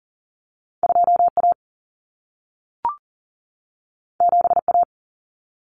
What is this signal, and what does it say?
Morse code, heard as a keyed tone, about 40 wpm, 700 Hz. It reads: 2A 7A
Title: Common Cut Numbers - 2-Digit 40wpm